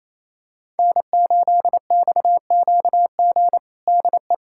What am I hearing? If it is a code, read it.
Morse code, heard as a keyed tone, about 28 wpm, 700 Hz. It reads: N8XQG DE
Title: USA Callsign DE Callsign - 5 Characters - 28wpm